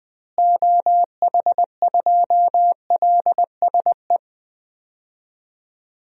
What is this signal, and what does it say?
Morse code, heard as a keyed tone, about 20 wpm, 700 Hz. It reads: OH2LSE
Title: DXCC-100 Call Signs - 6 Characters - Repeated 3x - 20wpm